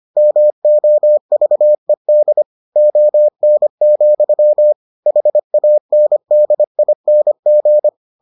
Morse code, heard as a keyed tone, about 25 wpm, 600 Hz. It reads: MOVED ON, HANDING